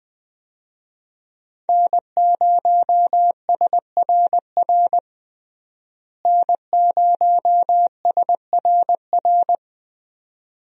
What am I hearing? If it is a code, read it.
Morse code, heard as a keyed tone, about 20 wpm, 700 Hz. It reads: N0SRR N0SRR
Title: USA Call Signs - 5 Characters - Repeated 2x - 20wpm